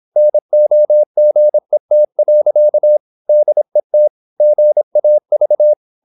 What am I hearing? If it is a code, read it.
Morse code, heard as a keyed tone, about 26 wpm, 600 Hz. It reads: NOGET. DET GAV
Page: Ferie 26 wpm